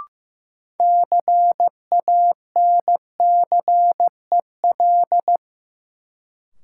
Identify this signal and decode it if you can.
Morse code, heard as a keyed tone, about 15 wpm, 700 Hz. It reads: CANCEL
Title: Sentences from Top 1200 Words - New Word Drill - 15wpm